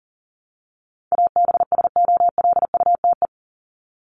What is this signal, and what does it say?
Morse code, heard as a keyed tone, about 40 wpm, 700 Hz. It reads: ABSOLUTE
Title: LY Suffix - Top 100 Words with Base Words 40wpm